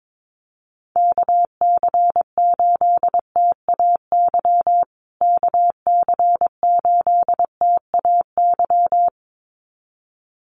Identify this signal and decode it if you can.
Morse code, heard as a keyed tone, about 22 wpm, 700 Hz. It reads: KC8TAY KC8TAY
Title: USA Call Signs Repeated 2x 22wpm